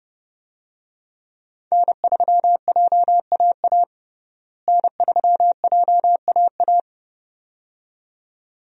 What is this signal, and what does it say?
Morse code, heard as a keyed tone, about 30 wpm, 700 Hz. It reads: N3JAA N3JAA
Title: USA Call Signs - 5 Characters - Repeated 2x - 30wpm